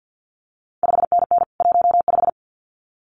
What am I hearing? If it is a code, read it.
Morse code, heard as a keyed tone, about 50 wpm, 700 Hz. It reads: 5NN 15
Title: CQ WW DX Contest Exchange - 50wpm